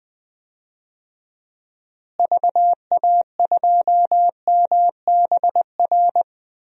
Morse code, heard as a keyed tone, about 21 wpm, 700 Hz. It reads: VA2MBR
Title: DXCC-100 Call Signs - 6 Characters - Repeated 3x - 20wpm